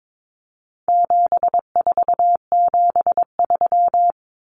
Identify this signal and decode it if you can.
Morse code, heard as a keyed tone, about 22 wpm, 700 Hz. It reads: 7473
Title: Common Cut Numbers - 4-Digit 22wpm